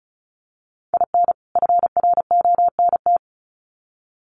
Morse code, heard as a keyed tone, about 34 wpm, 700 Hz. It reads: IN FRONT
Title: Sets of 2 Words 35wpm